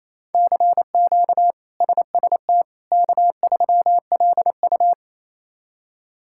Morse code, heard as a keyed tone, about 28 wpm, 700 Hz. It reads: CQ SST K3LU
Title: K1USN SST QSO Practice - 28wpm